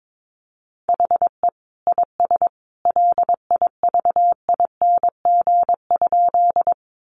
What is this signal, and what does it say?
Morse code, heard as a keyed tone, about 22 wpm, 700 Hz. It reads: HE IS LIVING?
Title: Sentences from Top 1100 Words 22wpm